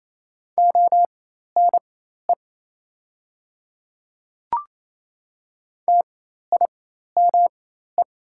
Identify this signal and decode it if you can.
Morse code, heard as a keyed tone, about 27 wpm, spaced as for 13 wpm, 690 Hz. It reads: ONE TIME
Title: 4x Character Spacing - Top 100 Words - 28wpm